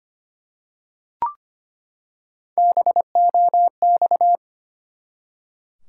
Morse code, heard as a keyed tone, about 25 wpm, 700 Hz. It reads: BOX
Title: Three Letter Words - 25wpm